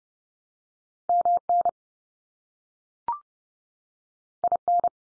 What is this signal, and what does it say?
Morse code, heard as a keyed tone, about 30 wpm, 700 Hz. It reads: MN IN
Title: US States 30WPM